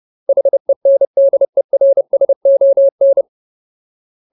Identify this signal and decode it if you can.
Morse code, heard as a keyed tone, about 30 wpm, 550 Hz. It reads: HENDERSON
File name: Ultimate Word List / 30WPM Ultimate